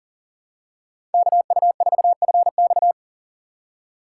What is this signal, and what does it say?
Morse code, heard as a keyed tone, about 40 wpm, 700 Hz. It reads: KU4FX
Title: USA Call Signs - 5 Characters - 6x Character Spacing - 40wpm